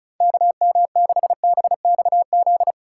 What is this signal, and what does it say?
Morse code, heard as a keyed tone, about 34 wpm, 700 Hz. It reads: KM6BXZ
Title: USA Call Signs - 4x Character Spacing - 35wpm